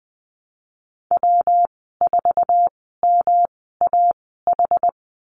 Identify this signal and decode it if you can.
Morse code, heard as a keyed tone, about 20 wpm, spaced as for 15 wpm, 700 Hz. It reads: W4MAH
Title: 140 Lesson 23g - Letter f - Call Signs 20wpm